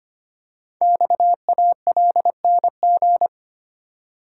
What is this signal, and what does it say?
Morse code, heard as a keyed tone, about 25 wpm, 700 Hz. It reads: XALNG
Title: Code Groups - 5 Letters - 25wpm